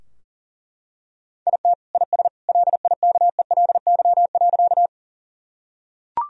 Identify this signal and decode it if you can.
Morse code, heard as a keyed tone, about 40 wpm, 700 Hz. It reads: IT IS LIKELY.
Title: Sentences from Top 1000 Words - Review Part 4 - 40wpm